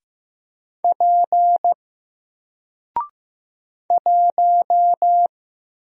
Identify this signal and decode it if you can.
Morse code, heard as a keyed tone, about 15 wpm, 700 Hz. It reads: P 1
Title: Single Letter-Number 15wpm